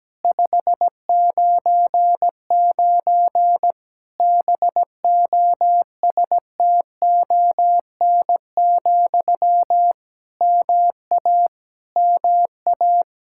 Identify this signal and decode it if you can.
Morse code, heard as a keyed tone, about 17 wpm, 700 Hz. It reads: 599 BOSTON, MA MA
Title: Practice QSO - Exchange 1 - Beginning - 17wpm